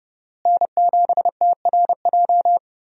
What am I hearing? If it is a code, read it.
Morse code, heard as a keyed tone, about 30 wpm, 700 Hz. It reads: N7TRJ
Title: USA Call Signs - 5 Characters - Repeated 3x - 30wpm